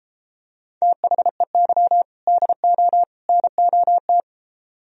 Morse code, heard as a keyed tone, about 33 wpm, 700 Hz. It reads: THEY DO NOT